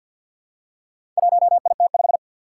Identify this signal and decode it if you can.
Morse code, heard as a keyed tone, about 52 wpm, 700 Hz. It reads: 1IT5
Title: Four Challenging Letter-Number Combinations 50wpm